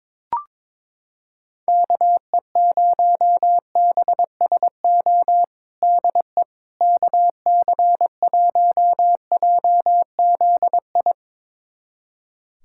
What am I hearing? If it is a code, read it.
Morse code, heard as a keyed tone, about 22 wpm, 700 Hz. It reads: KE0BSO DE KC1JZI
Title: USA Callsign DE Callsign - 6 Characters - 22wpm